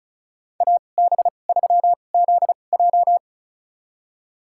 Morse code, heard as a keyed tone, about 36 wpm, spaced as for 25 wpm, 700 Hz. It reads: AB3ZJ
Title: USA Call Signs - 5 Characters - 2x Character Spacing - 35wpm